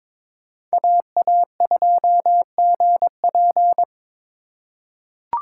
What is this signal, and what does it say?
Morse code, heard as a keyed tone, about 21 wpm, 700 Hz. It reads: AA2GP
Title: USA Call Signs - 5 Characters - 3x Character Spacing - 22wpm